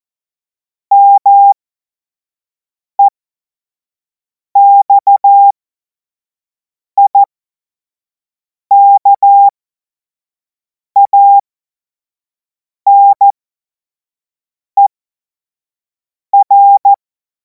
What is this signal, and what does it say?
Morse code, heard as a keyed tone, about 14 wpm, spaced as for 5 wpm, 800 Hz. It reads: MEXIKANER
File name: Morsekurs / Witze / Witz 05. Mai 2026 25 F70